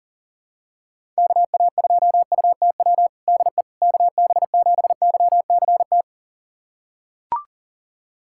Morse code, heard as a keyed tone, about 40 wpm, 700 Hz. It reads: KA2UTW DE KB7YCT
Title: USA Callsign DE Callsign 40wpm